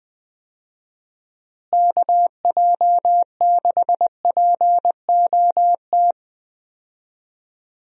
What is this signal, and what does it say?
Morse code, heard as a keyed tone, about 21 wpm, 700 Hz. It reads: KJ6POT
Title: USA Call Signs - 6 Characters - Repeated 2x - 20wpm